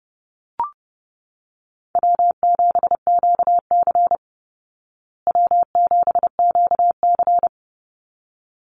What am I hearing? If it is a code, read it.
Morse code, heard as a keyed tone, about 30 wpm, 700 Hz. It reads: W7QC W7QC
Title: Easy Call Signs Repeated 2x - 30wpm